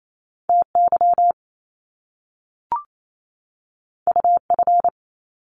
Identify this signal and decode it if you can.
Morse code, heard as a keyed tone, about 28 wpm, 700 Hz. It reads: T Y U F
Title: Top Two Letter Combinations in Words 28wpm